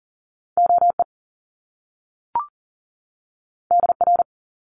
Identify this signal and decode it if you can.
Morse code, heard as a keyed tone, about 39 wpm, 700 Hz. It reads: OE DR
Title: Top Two Letter Combinations in Words 40wpm